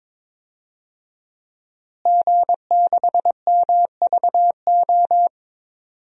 Morse code, heard as a keyed tone, about 22 wpm, 700 Hz. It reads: G6MVO
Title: DXCC-100 Call Signs - 5 Characters - Repeated 2x - 22wpm